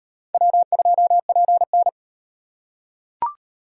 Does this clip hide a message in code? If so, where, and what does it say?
Yes: Morse code from 0.3 to 1.9 s, heard as a keyed tone, about 36 wpm, 700 Hz. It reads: W2PN